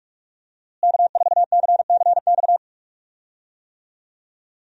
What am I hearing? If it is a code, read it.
Morse code, heard as a keyed tone, about 46 wpm, 700 Hz. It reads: K4CCX
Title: USA Call Signs - 5 Characters - Repeated 3x - 45wpm